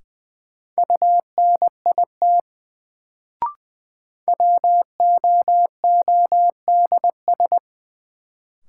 Morse code, heard as a keyed tone, about 20 wpm, 700 Hz. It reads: UNIT WOODS
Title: Sentences from Top 1000 Words - Review Part 2 - New Word Drill - 20wpm